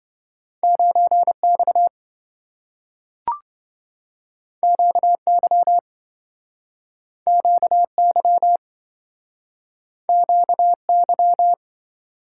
Morse code, heard as a keyed tone, about 27 wpm, 700 Hz. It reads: 9X QY QY QY